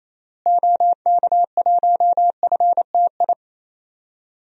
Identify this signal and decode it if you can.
Morse code, heard as a keyed tone, about 28 wpm, 700 Hz. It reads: OK1FTI
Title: DXCC-100 Call Signs - 6 Characters - Repeated 2x - 28wpm